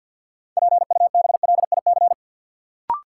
Morse code, heard as a keyed tone, about 53 wpm, 700 Hz. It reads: PUBLIC
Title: Sets of 2 Words - New Word Drill - 50wpm